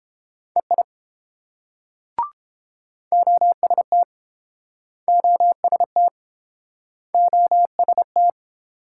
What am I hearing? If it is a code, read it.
Morse code, heard as a keyed tone, about 29 wpm, 700 Hz. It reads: EI OST OST OST